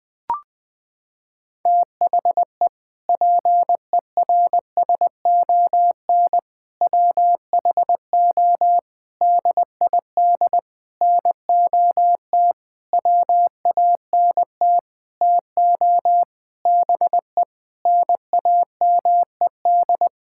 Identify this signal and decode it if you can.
Morse code, heard as a keyed tone, about 20 wpm, 700 Hz. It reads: THE PERSON WHO DID NOT WANT TO BE NAMED
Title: Sentences with 7-Word Phrases - 20wpm